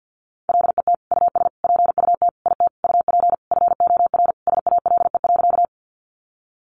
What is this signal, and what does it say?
Morse code, heard as a keyed tone, about 50 wpm, 700 Hz. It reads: LET US PUT IT UP FOR SALE.
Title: Sentences from Top 1400 Words 50wpm